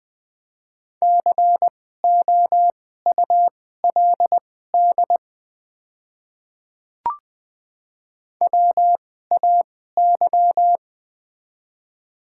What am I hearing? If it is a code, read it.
Morse code, heard as a keyed tone, about 20 wpm, spaced as for 14 wpm, 700 Hz. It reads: COULD WAY
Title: 2x Character Spacing - Top 100 Words - 20wpm